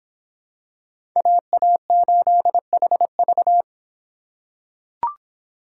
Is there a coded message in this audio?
Yes: Morse code from 1.2 to 3.6 s, heard as a keyed tone, about 23 wpm, 700 Hz. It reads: AA8HV